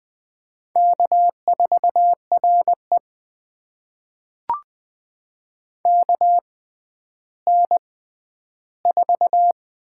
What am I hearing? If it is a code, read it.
Morse code, heard as a keyed tone, about 20 wpm, spaced as for 11 wpm, 700 Hz. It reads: K4RE KN4
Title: USA Call Signs - 6x Character Spacing - 20wpm